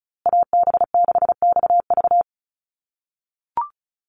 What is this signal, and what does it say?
Morse code, heard as a keyed tone, about 34 wpm, 700 Hz. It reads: AB6XV
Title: USA Call Signs Repeated 2x 35wpm